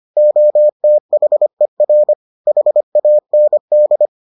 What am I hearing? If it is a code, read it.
Morse code, heard as a keyed tone, about 25 wpm, 600 Hz. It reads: OTHER HAND